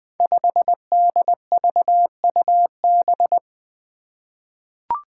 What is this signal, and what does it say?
Morse code, heard as a keyed tone, about 20 wpm, 700 Hz. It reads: 5DVUB
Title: Five Challenging Letter-Number Combinations 20wpm